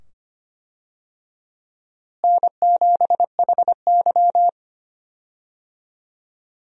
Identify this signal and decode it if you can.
Morse code, heard as a keyed tone, about 25 wpm, 700 Hz. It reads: N7HY
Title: USA Call Signs - 4 Characters - Repeated 2x - 25wpm